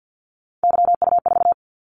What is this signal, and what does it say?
Morse code, heard as a keyed tone, about 50 wpm, 700 Hz. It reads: YU4
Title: Callsign Prefixes for Top 25 DXCC - 50wpm